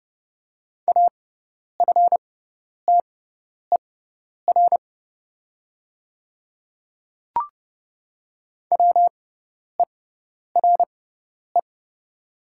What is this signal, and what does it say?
Morse code, heard as a keyed tone, about 30 wpm, spaced as for 10 wpm, 700 Hz. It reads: AFTER WERE
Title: 6x Character Spacing - Top 200 Words - 30wpm